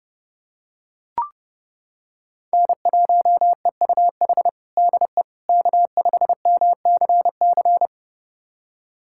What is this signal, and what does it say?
Morse code, heard as a keyed tone, about 32 wpm, 700 Hz. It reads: N1EUH DE K5MCC
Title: USA Callsign DE Callsign - 5 Characters - 30wpm